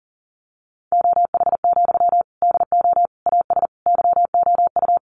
Morse code, heard as a keyed tone, about 40 wpm, 700 Hz. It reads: OH, DO AS YOU
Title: Sentences from Top 500 Words 40wpm